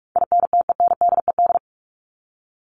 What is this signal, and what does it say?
Morse code, heard as a keyed tone, about 44 wpm, 700 Hz. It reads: INTENDED
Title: ED Suffix - Top 100 Words with Base Words 45wpm